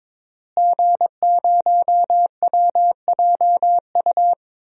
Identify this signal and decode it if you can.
Morse code, heard as a keyed tone, about 22 wpm, 700 Hz. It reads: G0WJU
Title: DXCC-100 Call Signs - 5 Characters - Repeated 3x - 22wpm